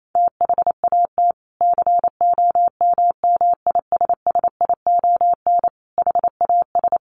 Morse code, heard as a keyed tone, about 28 wpm, 700 Hz. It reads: THAT COMMISSION HAS
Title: Sentences from Top 1400 Words 28wpm